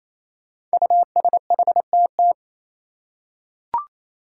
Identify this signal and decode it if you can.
Morse code, heard as a keyed tone, about 28 wpm, 700 Hz. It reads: USHTT
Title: Five Challenging Letter-Number Combinations 28wpm